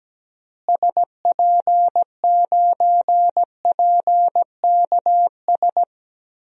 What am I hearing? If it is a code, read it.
Morse code, heard as a keyed tone, about 17 wpm, 700 Hz. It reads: SP9PKS
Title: DXCC-100 Call Signs - 6 Characters - Repeated 2x - 17wpm